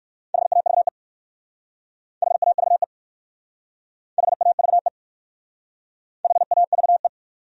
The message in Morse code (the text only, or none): HAVE HAVE HAVE HAVE